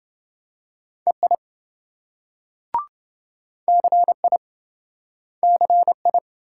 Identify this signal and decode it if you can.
Morse code, heard as a keyed tone, about 27 wpm, 700 Hz. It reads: EI CI CI